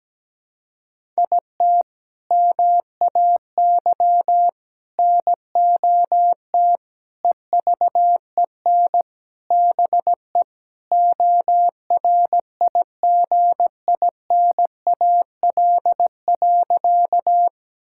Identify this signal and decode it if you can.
Morse code, heard as a keyed tone, about 17 wpm, 700 Hz. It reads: IT MAY NOT EVEN BE ORIGINAL.
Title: Sentences from Top 1000 Words - Review Part 6 - 17wpm